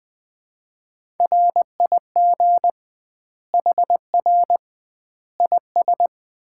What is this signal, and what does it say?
Morse code, heard as a keyed tone, about 20 wpm, 700 Hz. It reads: RIG HR IS
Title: Common QSO Phrases - 2x Word-Spacing - 20wpm